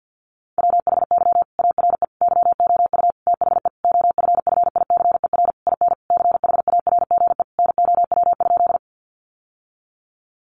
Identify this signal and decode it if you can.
Morse code, heard as a keyed tone, about 50 wpm, 700 Hz. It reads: WHY ARE YOU THE OFFICER IN CHARGE NOW?
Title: Sentences from Top 1200 Words 50wpm